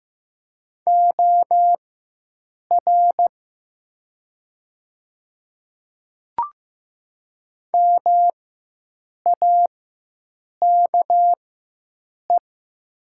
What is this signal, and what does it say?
Morse code, heard as a keyed tone, about 15 wpm, spaced as for 7 wpm, 700 Hz. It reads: OR MAKE
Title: 4x Character Spacing - Top 100 Words - 15wpm